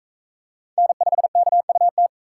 Morse code, heard as a keyed tone, about 40 wpm, 700 Hz. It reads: N5KUT